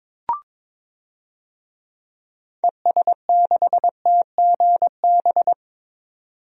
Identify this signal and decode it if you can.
Morse code, heard as a keyed tone, about 21 wpm, 700 Hz. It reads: ES6TGB
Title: DXCC-100 Call Signs - 6 Characters - 22wpm